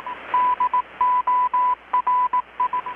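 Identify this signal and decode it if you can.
Morse code, heard as a keyed tone, about 18 wpm, 1000 Hz. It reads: DORI
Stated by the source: Morse from KPH's Night of Nights XX maritime radiotelegraphy event, received in Chiapas, Mexico about 3,800 km away on a Kenwood R-600 with a grounded 10 m long-wire antenna.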